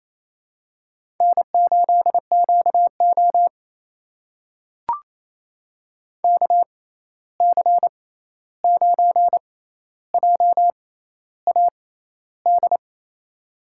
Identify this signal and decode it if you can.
Morse code, heard as a keyed tone, about 28 wpm, spaced as for 13 wpm, 700 Hz. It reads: N8QO KC9JAD
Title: USA Call Signs - 6x Character Spacing - 28wpm